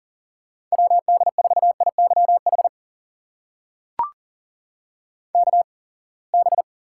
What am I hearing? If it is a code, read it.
Morse code, heard as a keyed tone, about 40 wpm, spaced as for 27 wpm, 700 Hz. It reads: WD4IYH KB
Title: USA Call Signs - 6 Characters - 8x Character Spacing - 40wpm